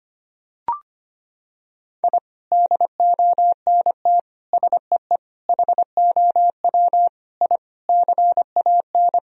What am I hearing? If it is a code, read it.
Morse code, heard as a keyed tone, about 25 wpm, 700 Hz. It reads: I DONT SEE HOW I CAN
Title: Sentences from Top 1000 Words - Review Part 2 - 25wpm